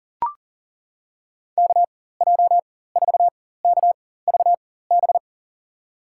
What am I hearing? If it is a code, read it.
Morse code, heard as a keyed tone, about 40 wpm, spaced as for 19 wpm, 700 Hz. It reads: KJ4KVB
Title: USA Call Signs - 6 Characters - 4x Character Spacing - 40wpm